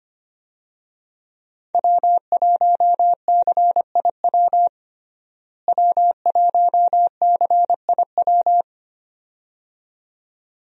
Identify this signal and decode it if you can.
Morse code, heard as a keyed tone, about 25 wpm, 700 Hz. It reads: W1CIW W1CIW
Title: USA Call Signs - 5 Characters - Repeated 2x - 25wpm